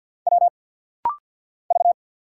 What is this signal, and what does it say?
Morse code, heard as a keyed tone, about 50 wpm, 700 Hz. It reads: W V
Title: Single Letters - Rapid-Fire 50wpm